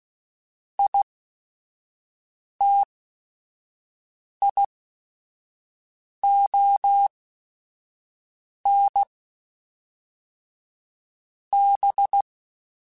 Morse code, heard as a keyed tone, about 16 wpm, spaced as for 5 wpm, 800 Hz. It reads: ITION B
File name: Technical Subjects / Technical Subjects-mp3s / XC @5